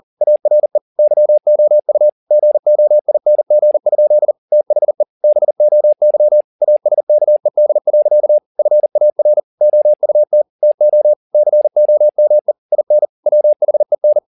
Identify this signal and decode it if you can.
Morse code, heard as a keyed tone, about 40 wpm, 600 Hz. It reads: ARE YOU GOING? THE BOY ASKED. FAR OUT TO COME IN WHEN
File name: the old man and the sea 1 40wpm